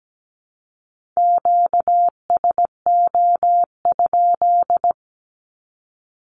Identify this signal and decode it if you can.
Morse code, heard as a keyed tone, about 17 wpm, 700 Hz. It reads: QSO?
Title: Q-Codes 17wpm